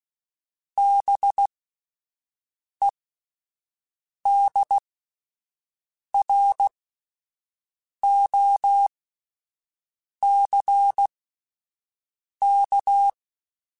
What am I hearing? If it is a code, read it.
Morse code, heard as a keyed tone, about 16 wpm, spaced as for 5 wpm, 800 Hz. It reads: BEDROCK